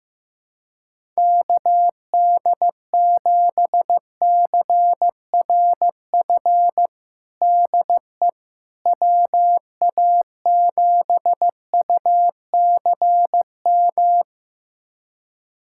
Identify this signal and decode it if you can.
Morse code, heard as a keyed tone, about 15 wpm, 700 Hz. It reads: KD7CRF DE WA7UCM
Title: USA Callsign DE Callsign - 6 Characters - 15wpm